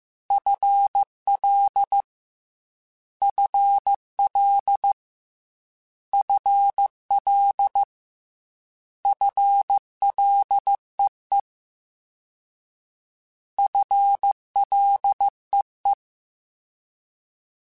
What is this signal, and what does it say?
Morse code, heard as a keyed tone, about 15 wpm, 800 Hz. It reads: FL FL FL FLEE FLEE
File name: FL / FL mp3s / FL Prefixes 3x2 Twice @13